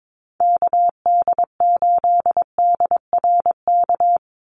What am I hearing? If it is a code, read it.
Morse code, heard as a keyed tone, about 22 wpm, 700 Hz. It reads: KD8DRK
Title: USA Call Signs 22wpm